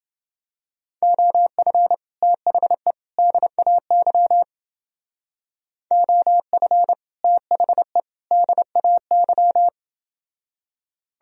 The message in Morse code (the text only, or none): OF THE DAY OF THE DAY